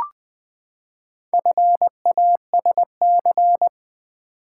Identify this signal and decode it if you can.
Morse code, heard as a keyed tone, about 20 wpm, 700 Hz. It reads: FASC
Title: Build A Word - 4 Letters At A Time - 11-Letter Words - 20wpm